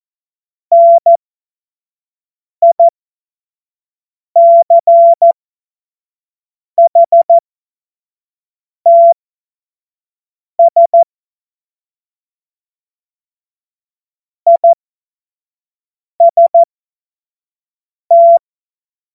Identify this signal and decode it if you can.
Morse code, heard as a keyed tone, about 14 wpm, spaced as for 5 wpm, 680 Hz. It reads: NICHTS IST
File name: Morsekurs / Witze / Witz 21. Dezember 2021, 50 F70